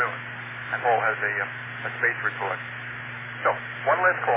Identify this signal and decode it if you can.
Emergency Alert System (SAME) tones, heard as harsh buzzing bursts, with speech. It is heard from 0.3 to 3.3 s.